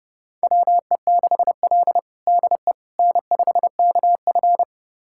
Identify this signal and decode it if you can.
Morse code, heard as a keyed tone, about 30 wpm, 700 Hz. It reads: WE6L DE N5KF
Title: USA Callsign DE Callsign - 4 Characters - 30wpm